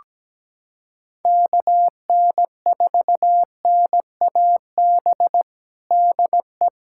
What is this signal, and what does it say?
Morse code, heard as a keyed tone, about 17 wpm, 700 Hz. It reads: KN4NAB DE
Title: USA Callsign DE Callsign - 6 Characters - 17wpm